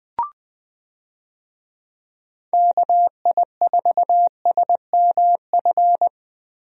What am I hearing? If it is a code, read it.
Morse code, heard as a keyed tone, about 20 wpm, 700 Hz. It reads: KI4SMF
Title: USA Call Signs - 6 Characters - Repeated 3x - 20wpm